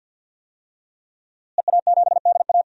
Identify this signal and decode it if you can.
Morse code, heard as a keyed tone, about 52 wpm, 700 Hz. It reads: EA7DA
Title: DXCC-100 Call Signs - 5 Characters - Repeated 3x - 50wpm